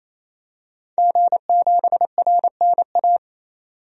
Morse code, heard as a keyed tone, about 28 wpm, 700 Hz. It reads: G7RNA
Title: DXCC-100 Call Signs - 5 Characters - 28wpm